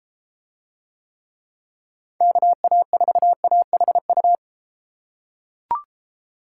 Morse code, heard as a keyed tone, about 31 wpm, 700 Hz. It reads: KA4AHU